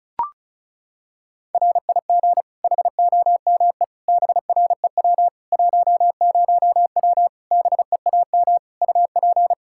Morse code, heard as a keyed tone, about 34 wpm, 700 Hz. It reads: RIG HOME BREW 10W BEAM UP
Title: Practice QSO - Exchange 2 - Info - 35wpm